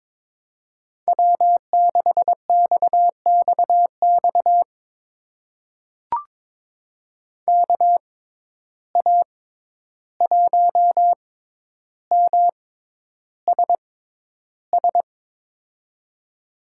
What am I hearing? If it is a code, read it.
Morse code, heard as a keyed tone, about 22 wpm, spaced as for 11 wpm, 700 Hz. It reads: W6XXX KA1MSS
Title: USA Call Signs - 6x Character Spacing - 22wpm